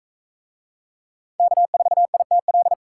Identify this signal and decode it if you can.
Morse code, heard as a keyed tone, about 42 wpm, 700 Hz. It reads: K4ITL